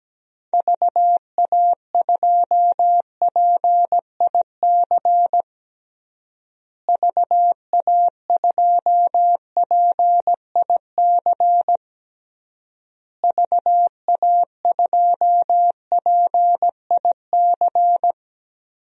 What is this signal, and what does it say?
Morse code, heard as a keyed tone, about 17 wpm, 700 Hz. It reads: VA2PIC VA2PIC VA2PIC
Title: DXCC-100 Call Signs - 6 Characters - Repeated 3x - 17wpm